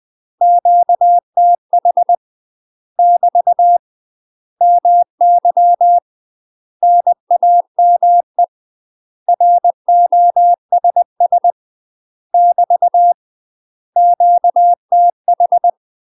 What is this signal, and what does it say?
Morse code, heard as a keyed tone, about 20 wpm, 700 Hz. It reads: QTH <BT> MY NAME ROSS <BT> QTH
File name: UK MASTER QSOs TRAINING / 20WPM / UK MASTER QSOs 0059